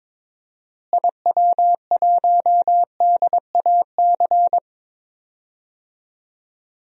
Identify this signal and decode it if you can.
Morse code, heard as a keyed tone, about 22 wpm, 700 Hz. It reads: IW1DAC
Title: DXCC-100 Call Signs - 6 Characters - Repeated 2x - 22wpm